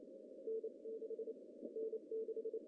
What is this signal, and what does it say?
Morse code, heard as a keyed tone, about 28 wpm, 440 Hz. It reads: N6 N6
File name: CW / cqww-cw-k5tr